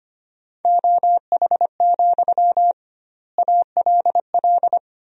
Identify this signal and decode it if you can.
Morse code, heard as a keyed tone, about 25 wpm, 700 Hz. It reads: OH, ALL
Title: 2x Word-Spacing - Sentences from Top 500 Words - 25wpm